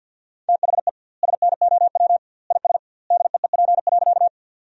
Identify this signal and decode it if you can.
Morse code, heard as a keyed tone, about 50 wpm, 700 Hz. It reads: THE SNOW IS DEEP.
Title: 2x Word-Spacing - Sentences from Top 500 Words - 50wpm